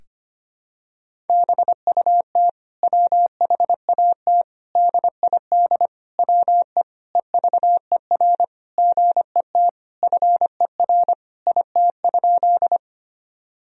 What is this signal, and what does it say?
Morse code, heard as a keyed tone, about 25 wpm, 700 Hz. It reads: BUT WHAT DID WE EVER GET FER IT?
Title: Sentences from Top 400 Words and Abbreviations 25wpm